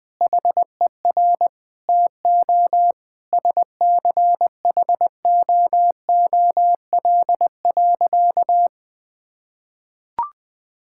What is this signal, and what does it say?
Morse code, heard as a keyed tone, about 20 wpm, 700 Hz. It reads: HER TO SCHOOL.
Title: Sentences from Top 1000 Words - Review Part 5 - 20wpm